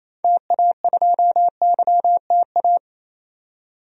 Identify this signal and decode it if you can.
Morse code, heard as a keyed tone, about 28 wpm, 700 Hz. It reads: TA2YTA
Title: DXCC-100 Call Signs - 6 Characters - 28wpm